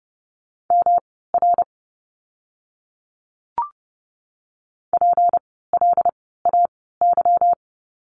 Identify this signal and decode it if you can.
Morse code, heard as a keyed tone, about 30 wpm, spaced as for 17 wpm, 700 Hz. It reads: MR PLAY
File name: course / 30:10wpm / 237 Lesson 38f - Letter q - Words 30wpm